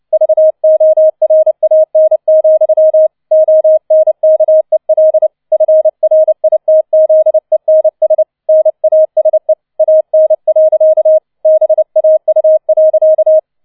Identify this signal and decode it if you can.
Morse code, heard as a keyed tone, about 29 wpm, 630 Hz. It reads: VORAN, ONKEL FRITZENS NASE AN. BAU.